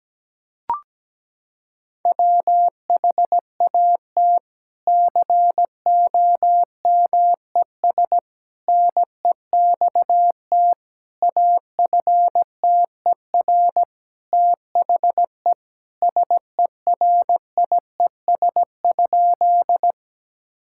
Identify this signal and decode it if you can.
Morse code, heard as a keyed tone, about 17 wpm, 700 Hz. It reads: WHAT COMES NEXT AFTER THE SERIES?
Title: Sentences from Top 1000 Words - Review Part 5 - 17wpm